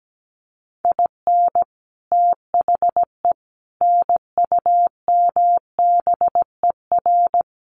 Morse code, heard as a keyed tone, about 17 wpm, 700 Hz. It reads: IN THE NUMBER
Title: Sets of 4 Words 17wpm